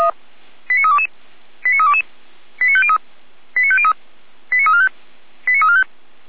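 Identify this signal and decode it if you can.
Five-tone selective call (ZVEI), heard as quick runs of five tones, tones of about 70 ms. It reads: 89322 89322 89503 89503 89345 89345
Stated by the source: ZVEI five-tone selective call 89230, received via SDR, recorded by Feuerwehr Orlamünde fire brigade.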